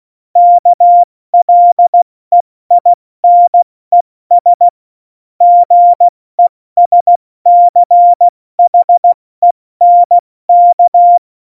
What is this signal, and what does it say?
Morse code, heard as a keyed tone, about 16 wpm, 700 Hz. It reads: KLEINES GESCHENK
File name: Morsekurs / Witze / Witz 20. Oktober 2020, 70 F80